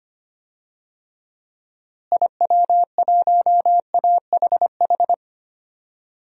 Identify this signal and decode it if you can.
Morse code, heard as a keyed tone, about 25 wpm, 700 Hz. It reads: IW1AHH
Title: DXCC-100 Call Signs - 6 Characters - Repeated 2x - 25wpm